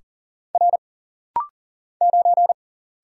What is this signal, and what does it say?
Morse code, heard as a keyed tone, about 40 wpm, 700 Hz. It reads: R 9
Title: Single Letter-Number - Rapid-Fire 40wpm